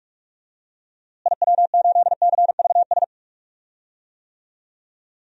Morse code, heard as a keyed tone, about 46 wpm, 700 Hz. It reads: IW8CVS
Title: DXCC-100 Call Signs - 6 Characters - Repeated 2x - 45wpm